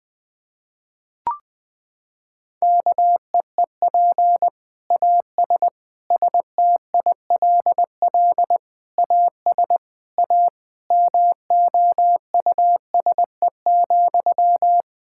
Morse code, heard as a keyed tone, about 20 wpm, 700 Hz. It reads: KEEP AS STILL AS A MOUSE,
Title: Sentences from Top 1000 Words - Review Part 5 - 20wpm